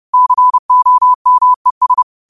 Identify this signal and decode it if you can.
Morse code, heard as a keyed tone, about 30 wpm, 1000 Hz. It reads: COMES